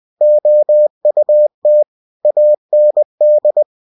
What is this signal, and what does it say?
Morse code, heard as a keyed tone, about 20 wpm, 600 Hz. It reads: OUT AND
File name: cw-books / winnie-the-pooh / winnie-20wpm-page-0040